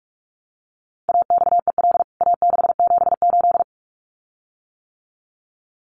Morse code, heard as a keyed tone, about 44 wpm, 700 Hz. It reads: AXEL A678
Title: CWT Exchange 45wpm